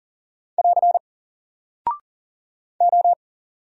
Morse code, heard as a keyed tone, about 39 wpm, 700 Hz. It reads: <AR> O
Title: Single Character (Letter:Number:Punctuation:Prosign) 40wpm